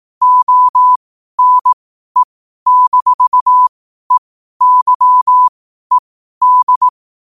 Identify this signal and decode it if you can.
Morse code, heard as a keyed tone, about 18 wpm, spaced as for 12 wpm, 1000 Hz. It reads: ONE-EYED
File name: qotdfiles / 12wpm